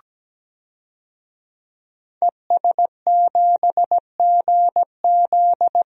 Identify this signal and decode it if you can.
Morse code, heard as a keyed tone, about 17 wpm, 700 Hz. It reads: ES7GZ
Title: DXCC-100 Call Signs - 5 Characters - Repeated 2x - 17wpm